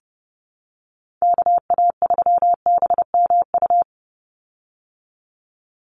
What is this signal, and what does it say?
Morse code, heard as a keyed tone, about 30 wpm, 700 Hz. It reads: KA3BMU
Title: Call Signs Repeated 2x - 30wpm